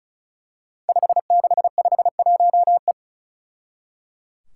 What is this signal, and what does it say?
Morse code, heard as a keyed tone, about 36 wpm, 700 Hz. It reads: 5651E
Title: Five Challenging Letter-Number Combinations 35wpm